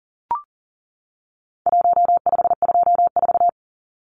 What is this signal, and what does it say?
Morse code, heard as a keyed tone, about 40 wpm, 700 Hz. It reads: 1524
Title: Numbers - 4-Digit 40wpm